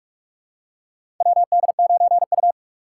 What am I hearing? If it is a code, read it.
Morse code, heard as a keyed tone, about 42 wpm, 700 Hz. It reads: WD9U
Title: USA Call Signs - 3x Character Spacing - 45wpm